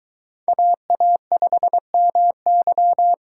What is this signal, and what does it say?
Morse code, heard as a keyed tone, about 22 wpm, 700 Hz. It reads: AA5MY